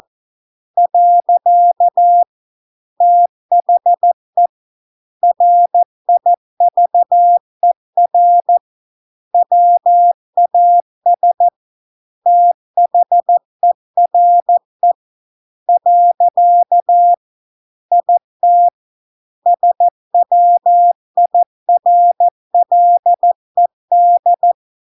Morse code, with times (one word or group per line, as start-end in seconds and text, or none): 0.8-2.2 .
3.0-4.5 THE
5.2-8.6 RIVER
9.3-11.5 WAS
12.3-14.9 THERE
15.7-17.1 .
17.9-18.7 IT
19.5-24.5 SWIRLED